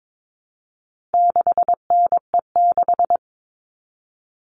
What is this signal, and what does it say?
Morse code, heard as a keyed tone, about 22 wpm, 700 Hz. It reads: 6NE6
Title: All Cut Numbers - 4-Digit 22wpm